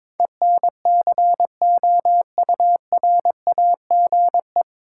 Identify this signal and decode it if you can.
Morse code, heard as a keyed tone, about 22 wpm, 700 Hz. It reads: ENCOURAGE
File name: 9 Letter Words - First 3 Letters Given - 22wpm